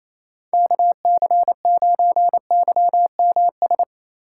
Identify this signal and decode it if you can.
Morse code, heard as a keyed tone, about 28 wpm, 700 Hz. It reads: KC9YMS
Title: USA Call Signs - 6 Characters - 6x Character Spacing - 28wpm